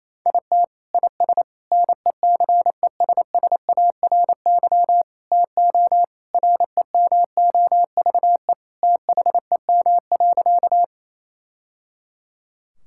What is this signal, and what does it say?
Morse code, heard as a keyed tone, about 28 wpm, 700 Hz. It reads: IT IS NECESSARY TO REMOVE THEM.
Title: Sentences from Top 1000 Words - Review Part 5 - 28wpm